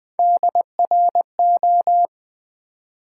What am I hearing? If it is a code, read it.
Morse code, heard as a keyed tone, about 20 wpm, 700 Hz. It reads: DRO
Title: Build A Word - 1 Letter At A Time - 5-Letter Words - 20wpm